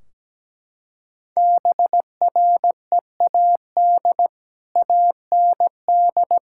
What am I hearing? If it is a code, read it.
Morse code, heard as a keyed tone, about 17 wpm, 700 Hz. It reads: BREAD AND
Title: Sentences from Top 1600 Words 17wpm